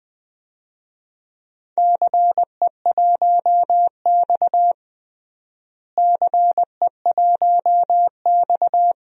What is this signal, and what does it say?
Morse code, heard as a keyed tone, about 20 wpm, 700 Hz. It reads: CE1X CE1X
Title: DXCC-100 Call Signs - 4 Characters - Repeated 2x - 20wpm